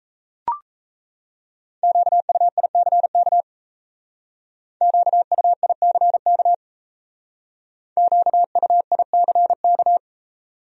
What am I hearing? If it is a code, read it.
Morse code, heard as a keyed tone, about 35 wpm, 700 Hz. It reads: QUICK QUICK QUICK